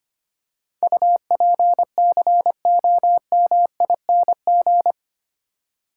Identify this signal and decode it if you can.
Morse code, heard as a keyed tone, about 25 wpm, 700 Hz. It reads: UPCOMING
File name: Eight Letter Words - 25wpm